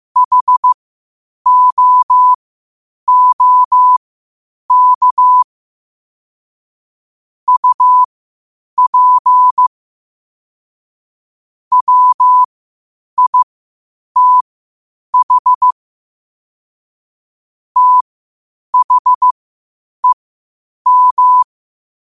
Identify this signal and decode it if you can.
Morse code, heard as a keyed tone, about 15 wpm, spaced as for 8 wpm, 1000 Hz. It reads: HOOK UP WITH THEM
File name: qotdfiles / 8wpm